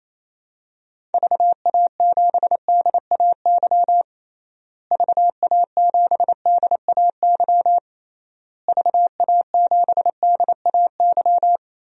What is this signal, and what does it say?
Morse code, heard as a keyed tone, about 28 wpm, 700 Hz. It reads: VA7DAY VA7DAY VA7DAY
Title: DXCC-100 Call Signs - 6 Characters - Repeated 3x - 28wpm